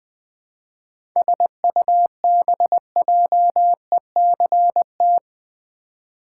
Morse code, heard as a keyed tone, about 21 wpm, 700 Hz. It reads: SUBJECT
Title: 7 Letter Words - First 3 Letters Given - 20wpm